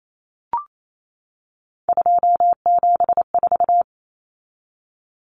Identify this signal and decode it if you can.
Morse code, heard as a keyed tone, about 28 wpm, 700 Hz. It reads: 274
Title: Numbers - 3-Digit 28wpm